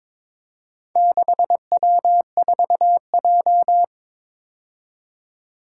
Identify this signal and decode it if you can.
Morse code, heard as a keyed tone, about 22 wpm, 700 Hz. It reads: 6W4J
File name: practice / 22wpm / Four Letter-Number Combinations - 22wpm